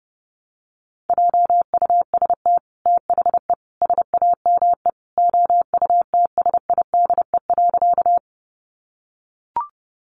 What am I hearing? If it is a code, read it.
Morse code, heard as a keyed tone, about 30 wpm, 700 Hz. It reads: JUST THE SAME OUTSIDE.
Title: Sentences from Top 900 Words 30wpm